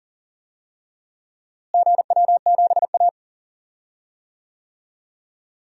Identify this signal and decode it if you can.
Morse code, heard as a keyed tone, about 40 wpm, 700 Hz. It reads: GW7A
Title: DXCC-100 Call Signs - 4 Characters - Repeated 2x - 40wpm